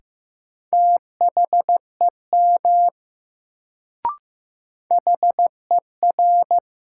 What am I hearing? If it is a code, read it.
Morse code, heard as a keyed tone, about 15 wpm, 700 Hz. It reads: THEM HER
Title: Sets of 2 Words - New Word Drill - 15wpm